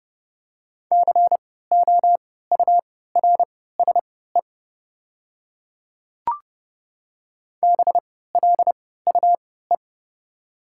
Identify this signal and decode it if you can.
Morse code, heard as a keyed tone, about 30 wpm, spaced as for 17 wpm, 690 Hz. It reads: COURSE BLUE
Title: 3x Character Spacing - Top 500 Words - 30wpm